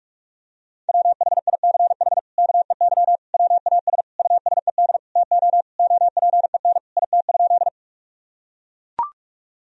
Morse code, heard as a keyed tone, about 41 wpm, 700 Hz. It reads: WHICH KEY WAS USED TO OPEN IT?
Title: Sentences from Top 1000 Words - Review Part 4 - 45wpm